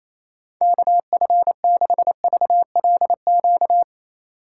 Morse code, heard as a keyed tone, about 28 wpm, 700 Hz. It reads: KF6VLQ
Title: USA Call Signs - 6 Characters - 4x Character Spacing - 28wpm